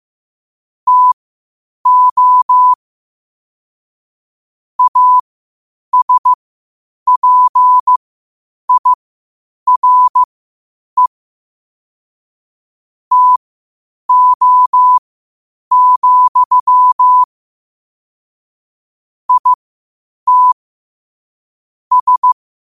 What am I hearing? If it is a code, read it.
Morse code, heard as a keyed tone, about 15 wpm, spaced as for 8 wpm, 1000 Hz. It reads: TO ASPIRE TO, IT S
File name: qotdfiles / 8wpm